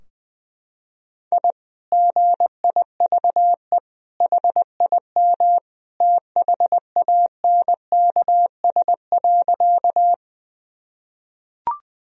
Morse code, heard as a keyed tone, about 21 wpm, 700 Hz. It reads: I GIVE HIM THANKS.
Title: Sentences from Top 1000 Words - Review Part 3 - 20wpm